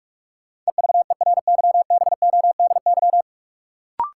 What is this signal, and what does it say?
Morse code, heard as a keyed tone, about 43 wpm, 700 Hz. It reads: EVERYBODY
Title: Sentences from Top 1400 Words - New Word Drill Alt - 45wpm